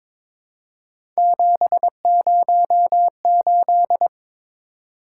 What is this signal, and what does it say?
Morse code, heard as a keyed tone, about 22 wpm, 700 Hz. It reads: 708
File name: practice / 22wpm / Three Letter-Number Combinations - 22wpm